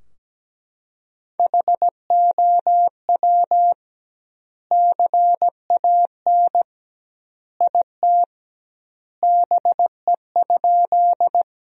Morse code, heard as a keyed tone, about 17 wpm, 700 Hz. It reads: HOW CAN IT BE?
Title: 2x Word-Spacing - Sentences from Top 100 Words - 17wpm